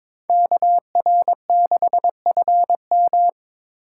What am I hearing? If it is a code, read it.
Morse code, heard as a keyed tone, about 22 wpm, 700 Hz. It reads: KR6FM
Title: USA Call Signs - 5 Characters - 4x Character Spacing - 22wpm